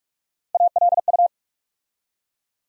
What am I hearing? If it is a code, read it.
Morse code, heard as a keyed tone, about 46 wpm, 700 Hz. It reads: ALU
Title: USA Call Sign Postfixes - 3 Letters 45wpm